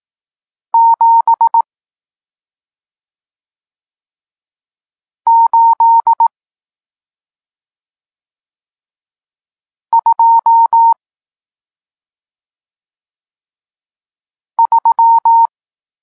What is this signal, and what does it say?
Morse code, heard as a keyed tone, about 18 wpm, 920 Hz. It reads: 7 8 2 3
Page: Tekst 40 – Nauka znaków przestankowych